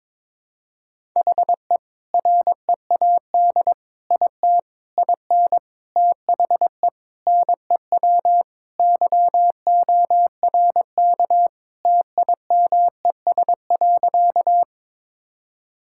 Morse code, heard as a keyed tone, about 22 wpm, 700 Hz. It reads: HE READ IT IN THE NEW YORK TIMES.
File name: practice / 22wpm / Sentences with 4-Word Phrases - 22wpm